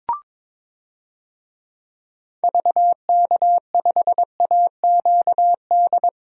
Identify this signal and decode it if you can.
Morse code, heard as a keyed tone, about 21 wpm, 700 Hz. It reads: VK5AQD
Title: DXCC-100 Call Signs - 6 Characters - 22wpm